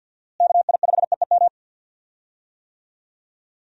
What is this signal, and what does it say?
Morse code, heard as a keyed tone, about 51 wpm, 700 Hz. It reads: KI5EEM
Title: USA Call Signs - 6 Characters - Repeated 2x - 50wpm